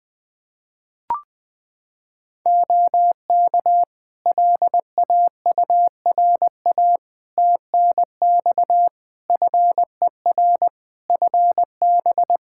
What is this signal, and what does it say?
Morse code, heard as a keyed tone, about 20 wpm, 700 Hz. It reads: OK LAURA TNX FER FB
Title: K1USN SST QSO Practice 20wpm